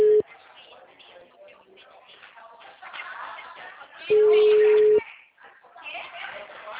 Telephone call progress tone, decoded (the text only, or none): ringback tone